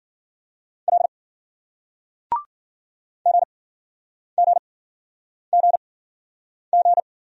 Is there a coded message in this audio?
Yes: Morse code from 0.9 to 7.0 s, heard as a keyed tone, about 50 wpm, spaced as for 9 wpm, 700 Hz. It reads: L GGGG